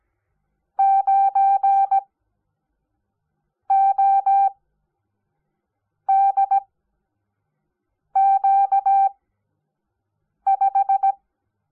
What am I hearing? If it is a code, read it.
Morse code, heard as a keyed tone, about 17 wpm, spaced as for 5 wpm, 780 Hz. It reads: 9ODQ5
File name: Onlinelehrgang / morsen / DJ4UF Morsen05